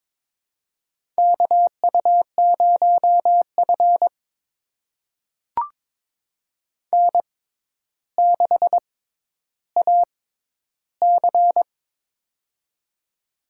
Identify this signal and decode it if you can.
Morse code, heard as a keyed tone, about 22 wpm, spaced as for 11 wpm, 700 Hz. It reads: KU0F N6AC
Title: USA Call Signs - 4 Characters - 6x Character Spacing - 22wpm